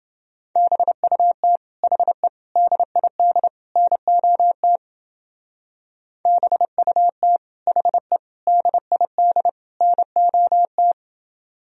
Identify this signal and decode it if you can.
Morse code, heard as a keyed tone, about 28 wpm, 700 Hz. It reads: BUT HE DID NOT BUT HE DID NOT